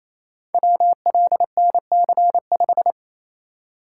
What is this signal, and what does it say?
Morse code, heard as a keyed tone, about 27 wpm, 700 Hz. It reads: WLNC5
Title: Code Groups - 5 Letter:Numbers - 28wpm